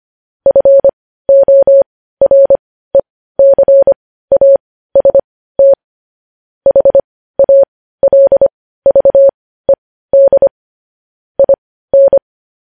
Morse code, heard as a keyed tone, about 25 wpm, spaced as for 15 wpm, 550 Hz. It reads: FORECAST HALVED IN
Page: News Headlines in Morse Code at 15 WPM